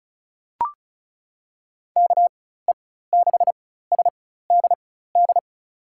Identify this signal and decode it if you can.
Morse code, heard as a keyed tone, about 34 wpm, spaced as for 16 wpm, 700 Hz. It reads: KE6SDD
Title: USA Call Signs - 6 Characters - 4x Character Spacing - 35wpm